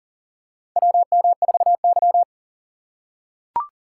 Morse code, heard as a keyed tone, about 40 wpm, 700 Hz. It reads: WM4Y
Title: USA Call Signs - 4 Characters - 4x Character Spacing - 40wpm